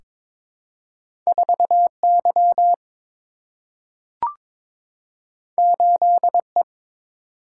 Morse code, heard as a keyed tone, about 22 wpm, 700 Hz. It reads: 4 Y 8 E
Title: Single-Letter Single-Number 22wpm